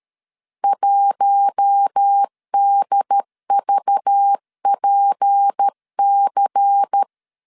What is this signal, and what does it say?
Morse code, heard as a keyed tone, about 13 wpm, 800 Hz. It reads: 1DVPC
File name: LEZIONI / 6 PKX0 / 6 P 12wpm 2033